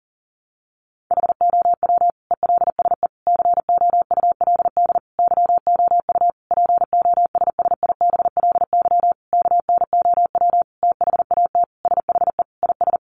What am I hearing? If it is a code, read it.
Morse code, heard as a keyed tone, about 40 wpm, 700 Hz. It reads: HOW ELSE COULD YOU POSSIBLY KNOW THAT SHE IS
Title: Sentences from Top 1300 Words 40wpm